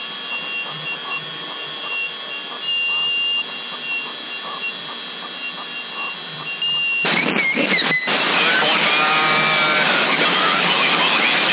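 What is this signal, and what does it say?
FT8.
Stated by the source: FT8 signal on 27075 kHz CB band, received via antennas in Sassari.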